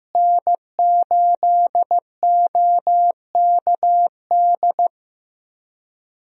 Morse code, heard as a keyed tone, about 15 wpm, 700 Hz. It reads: N8OKD
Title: USA Call Signs - 5 Characters - Repeated 2x - 15wpm